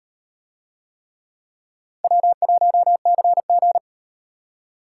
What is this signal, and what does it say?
Morse code, heard as a keyed tone, about 36 wpm, 700 Hz. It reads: W1CG